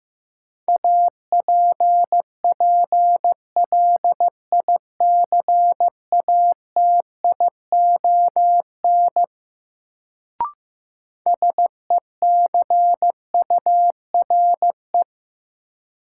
Morse code, heard as a keyed tone, about 15 wpm, 700 Hz. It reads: APPLICATION SECURE
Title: Sentences from Top 1200 Words - New Word Drill - 15wpm